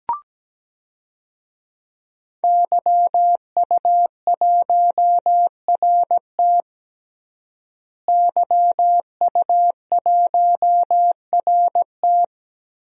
Morse code, heard as a keyed tone, about 17 wpm, 700 Hz. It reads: YU1RT YU1RT
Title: DXCC-100 Call Signs - 5 Characters - Repeated 3x - 17wpm